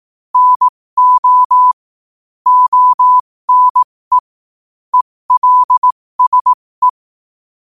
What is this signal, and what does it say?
Morse code, heard as a keyed tone, about 18 wpm, spaced as for 15 wpm, 1000 Hz. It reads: NO ONE ELSE
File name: qotdfiles / 15wpm